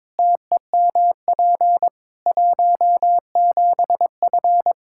Morse code, heard as a keyed tone, about 22 wpm, 700 Hz. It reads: TEMP 17F
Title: K1USN SST QSO Practice 22wpm